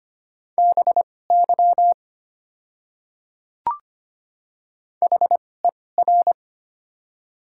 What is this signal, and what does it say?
Morse code, heard as a keyed tone, about 25 wpm, spaced as for 18 wpm, 690 Hz. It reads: BY HER
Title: 2x Character Spacing - Top 100 Words - 25wpm